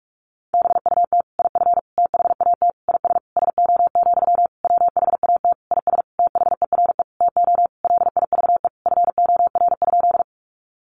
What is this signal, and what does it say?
Morse code, heard as a keyed tone, about 44 wpm, 700 Hz. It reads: BUT IF THAT IS SO, WHAT IS THERE TO LIVE FOR?
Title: Sentences from Top 200 Words 45wpm